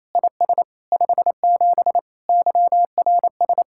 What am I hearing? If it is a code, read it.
Morse code, heard as a keyed tone, about 34 wpm, 700 Hz. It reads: IS 57 YRS
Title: Speed-Racing Common QSO Phrases - 28wpm